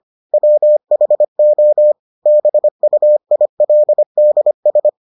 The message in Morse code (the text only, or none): WHO BUILDS